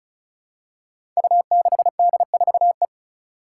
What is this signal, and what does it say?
Morse code, heard as a keyed tone, about 36 wpm, 700 Hz. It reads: U6D4E
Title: Five Challenging Letter-Number Combinations 35wpm